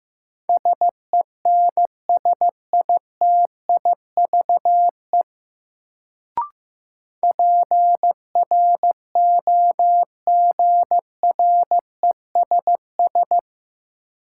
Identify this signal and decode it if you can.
Morse code, heard as a keyed tone, about 15 wpm, 700 Hz. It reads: SENSITIVE PROGRESS
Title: Sentences from Top 1400 Words - New Word Drill - 15wpm